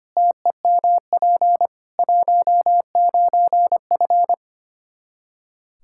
Morse code, heard as a keyed tone, about 25 wpm, 700 Hz. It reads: TEMP 19F
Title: Practice QSO - Exchange 2 - Info - 25wpm